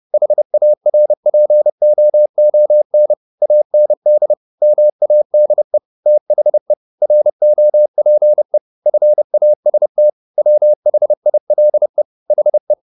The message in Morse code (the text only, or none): HARPOON AND MADE THE ROPE FAST WHILE HE